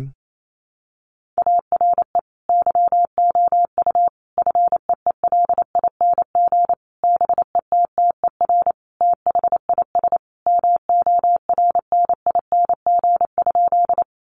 Morse code, heard as a keyed tone, about 28 wpm, 700 Hz. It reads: ARE YOU FEELING BETTER THIS MORNING?
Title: Sentences from Top 1200 Words 28wpm